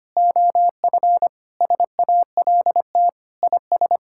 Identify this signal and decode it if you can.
Morse code, heard as a keyed tone, about 25 wpm, 700 Hz. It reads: OF SALT IS
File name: practice / 25wpm / Sentences with 4-Word Phrases - 25wpm